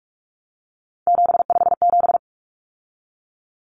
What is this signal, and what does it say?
Morse code, heard as a keyed tone, about 44 wpm, 700 Hz. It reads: 757
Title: Common Cut Numbers - 3-Digit 45wpm